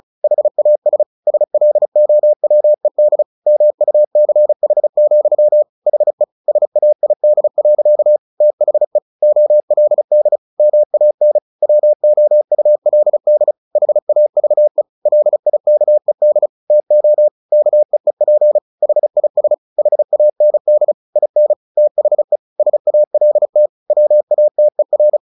Morse code, heard as a keyed tone, about 34 wpm, 600 Hz. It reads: HAS SLOWED MUCH, HE SAID. THE OLD MAN WOULD HAVE LIKED TO KEEP HIS HAND IN THE SALT WATER
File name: the old man and the sea / the old man and the sea 1 35wpm